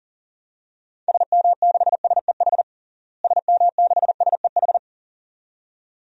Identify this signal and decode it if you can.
Morse code, heard as a keyed tone, about 40 wpm, 700 Hz. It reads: SM6SEH SM6SEH
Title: DXCC-100 Call Signs - 6 Characters - Repeated 2x - 40wpm